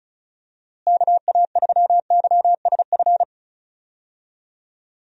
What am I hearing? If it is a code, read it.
Morse code, heard as a keyed tone, about 34 wpm, 700 Hz. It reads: KA3YSF
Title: USA Call Signs - 6 Characters - Repeated 3x - 35wpm